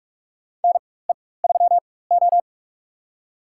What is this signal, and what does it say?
Morse code, heard as a keyed tone, about 44 wpm, spaced as for 21 wpm, 690 Hz. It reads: NE3O
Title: USA Call Signs - 4 Characters - 4x Character Spacing - 45wpm